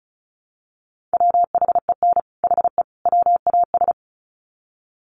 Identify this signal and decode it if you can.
Morse code, heard as a keyed tone, about 34 wpm, 700 Hz. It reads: WHEN HE WAS
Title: Sets of 3 Words 35wpm